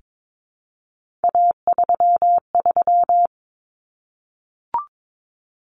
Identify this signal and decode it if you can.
Morse code, heard as a keyed tone, about 22 wpm, 700 Hz. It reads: A33
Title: Common Cut Numbers - 3-Digit 22wpm